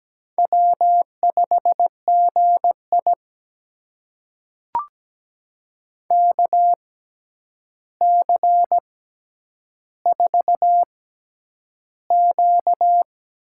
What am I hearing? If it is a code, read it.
Morse code, heard as a keyed tone, about 17 wpm, spaced as for 9 wpm, 700 Hz. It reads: W5GI KC4Q
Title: USA Call Signs - 6x Character Spacing - 17wpm